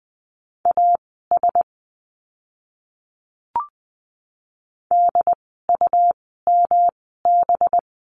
Morse code, heard as a keyed tone, about 20 wpm, spaced as for 15 wpm, 700 Hz. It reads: AS DUMB
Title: 185 Lesson 30f - Letter b - Words 20wpm